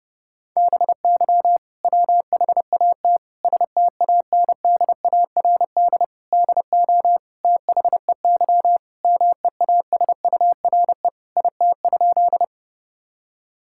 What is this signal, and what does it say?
Morse code, heard as a keyed tone, about 30 wpm, 700 Hz. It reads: BY WHAT STANDARD DO THEY MEASURE IT?
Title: Sentences from Top 1000 Words - Review Part 6 - 30wpm